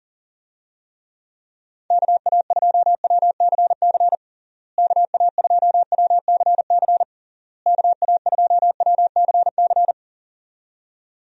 Morse code, heard as a keyed tone, about 37 wpm, 700 Hz. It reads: KA2WCC KA2WCC KA2WCC
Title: USA Call Signs - 6 Characters - Repeated 3x - 40wpm